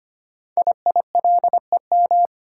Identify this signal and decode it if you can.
Morse code, heard as a keyed tone, about 25 wpm, 700 Hz. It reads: IILEM
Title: Code Groups - 5 Letter:Numbers - 25wpm